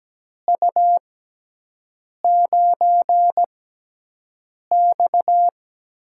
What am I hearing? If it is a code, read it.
Morse code, heard as a keyed tone, about 17 wpm, spaced as for 6 wpm, 700 Hz. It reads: U9X
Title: USA Call Signs - 4 Characters - 6x Character Spacing - 17wpm